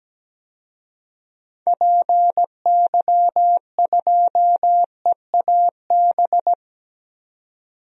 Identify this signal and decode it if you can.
Morse code, heard as a keyed tone, about 17 wpm, 700 Hz. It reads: PY2EAB
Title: DXCC-100 Call Signs - 6 Characters - 17wpm